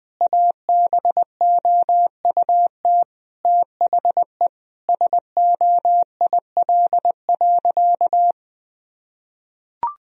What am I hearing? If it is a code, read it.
Morse code, heard as a keyed tone, about 21 wpm, 700 Hz. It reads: ABOUT THE SOIL.
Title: Sentences from Top 1000 Words - Review Part 4 - 20wpm